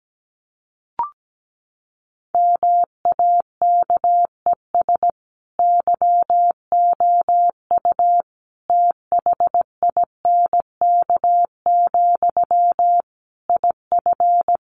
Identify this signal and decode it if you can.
Morse code, heard as a keyed tone, about 17 wpm, 700 Hz. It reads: MAKES YOU THINK, IF
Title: A Sound Of Thunder 17wpm